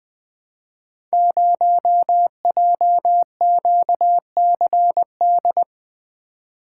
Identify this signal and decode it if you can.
Morse code, heard as a keyed tone, about 20 wpm, 700 Hz. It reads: 0JQCD
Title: Code Groups - 5 Letter:Numbers - 20wpm